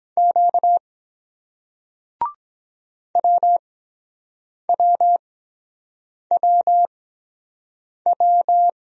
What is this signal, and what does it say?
Morse code, heard as a keyed tone, about 23 wpm, spaced as for 7 wpm, 700 Hz. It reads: Q WWWW